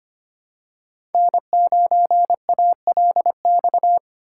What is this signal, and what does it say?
Morse code, heard as a keyed tone, about 25 wpm, 700 Hz. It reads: N9ALX
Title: USA Call Signs - 5 Characters - 8x Character Spacing - 25wpm